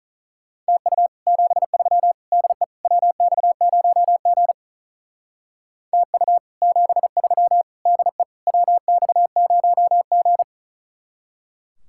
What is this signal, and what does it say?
Morse code, heard as a keyed tone, about 46 wpm, 700 Hz. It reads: TU 73 DE WX0G TU 73 DE WX0G